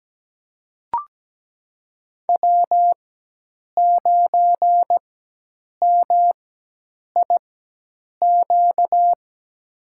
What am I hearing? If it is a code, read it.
Morse code, heard as a keyed tone, about 17 wpm, spaced as for 8 wpm, 700 Hz. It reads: W9MIQ
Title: USA Call Signs - 5 Characters - 4x Character Spacing - 17wpm